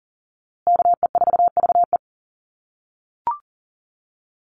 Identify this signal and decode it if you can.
Morse code, heard as a keyed tone, about 40 wpm, 700 Hz. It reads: KE4VE
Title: USA Call Signs Repeated 3x 40wpm